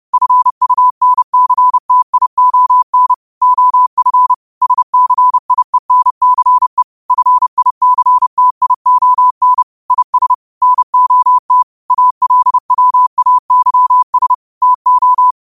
Morse code, heard as a keyed tone, about 30 wpm, 1000 Hz. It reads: FUNCTION OF SCIENCE FICTION IS NOT ALWAYS TO